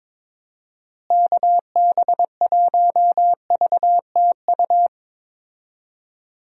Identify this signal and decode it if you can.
Morse code, heard as a keyed tone, about 22 wpm, 700 Hz. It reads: KB1VTU
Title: USA Call Signs - 6 Characters - Repeated 3x - 22wpm